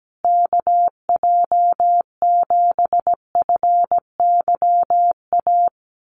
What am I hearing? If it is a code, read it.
Morse code, heard as a keyed tone, about 17 wpm, 700 Hz. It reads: KJ7FYA
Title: USA Call Signs Repeated 2x 17wpm